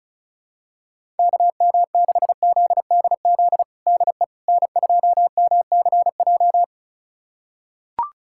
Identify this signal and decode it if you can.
Morse code, heard as a keyed tone, about 34 wpm, 700 Hz. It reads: KM6ZDZ DE N2MCJ
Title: USA Callsign DE Callsign 35wpm